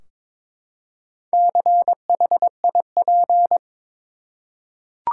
Morse code, heard as a keyed tone, about 22 wpm, 700 Hz. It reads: CHIP
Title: Four Letter Words - 22wpm